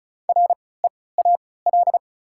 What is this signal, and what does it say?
Morse code, heard as a keyed tone, about 36 wpm, spaced as for 20 wpm, 700 Hz. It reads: REAL
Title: 3x Character Spacing - Top 300 Words - 35wpm